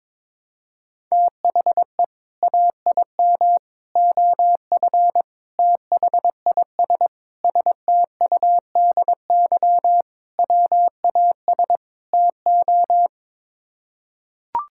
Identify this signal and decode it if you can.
Morse code, heard as a keyed tone, about 22 wpm, 700 Hz. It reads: THE AIM OF THIS STUDY WAS TO
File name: practice / 22wpm / Sets of 7 Words - 22wpm